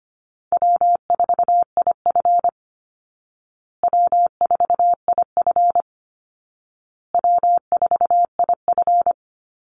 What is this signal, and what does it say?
Morse code, heard as a keyed tone, about 25 wpm, 700 Hz. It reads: W4IF W4IF W4IF
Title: Call Signs Repeated 2x - 25wpm